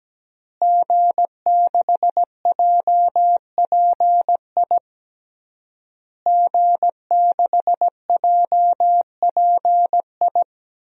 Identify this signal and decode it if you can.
Morse code, heard as a keyed tone, about 17 wpm, 700 Hz. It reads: G6JPI G6JPI
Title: DXCC-100 Call Signs - 5 Characters - Repeated 3x - 17wpm